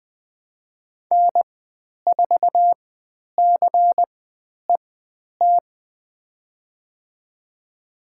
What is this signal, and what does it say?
Morse code, heard as a keyed tone, about 20 wpm, spaced as for 10 wpm, 700 Hz. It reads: N4CET
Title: 102 Lesson 17g - Letter u - Call Signs 20wpm